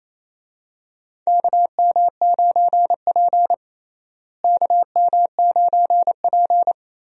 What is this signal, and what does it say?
Morse code, heard as a keyed tone, about 28 wpm, 700 Hz. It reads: KM9P KM9P
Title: USA Call Signs - 4 Characters - Repeated 2x - 28wpm